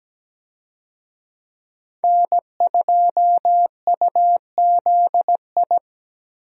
Morse code, heard as a keyed tone, about 17 wpm, 700 Hz. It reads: N2UZI
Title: USA Call Signs - 5 Characters - Repeated 2x - 17wpm